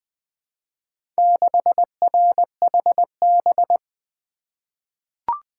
Morse code, heard as a keyed tone, about 20 wpm, 700 Hz. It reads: 6RHB
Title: Four Letter-Number Combinations - 20wpm